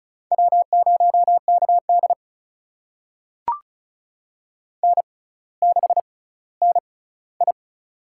Morse code, heard as a keyed tone, about 34 wpm, spaced as for 18 wpm, 700 Hz. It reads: W0KD N6NI
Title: USA Call Signs - 4 Characters - 6x Character Spacing - 35wpm